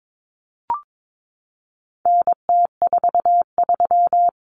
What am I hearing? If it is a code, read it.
Morse code, heard as a keyed tone, about 22 wpm, 700 Hz. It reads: NT43
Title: Common Cut Numbers - 4-Digit 22wpm